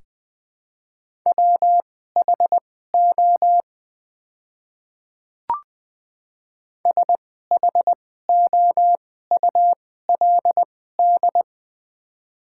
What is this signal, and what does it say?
Morse code, heard as a keyed tone, about 21 wpm, spaced as for 15 wpm, 690 Hz. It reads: WHO SHOULD
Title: 2x Character Spacing - Top 500 Words - 20wpm